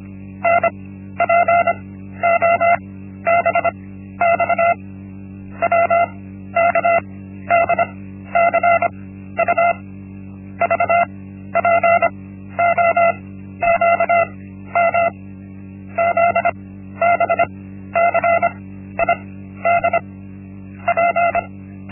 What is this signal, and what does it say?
Morse code, heard as a keyed tone, about 25 wpm, 680 Hz. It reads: N P O B X W K D C U V P O Q M Z B C I D P
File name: Linksmakalnis-1